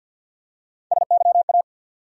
Morse code, heard as a keyed tone, about 52 wpm, 700 Hz. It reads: SYA
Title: USA Call Sign Postfixes - 3 Letters 50wpm